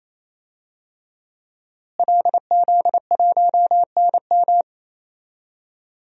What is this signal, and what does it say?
Morse code, heard as a keyed tone, about 28 wpm, 700 Hz. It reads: LZ1NM
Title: DXCC-100 Call Signs - 5 Characters - Repeated 3x - 28wpm